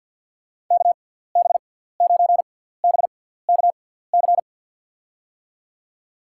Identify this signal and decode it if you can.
Morse code, heard as a keyed tone, about 53 wpm, spaced as for 17 wpm, 700 Hz. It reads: KB9BKC
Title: USA Call Signs - 6x Character Spacing - 50wpm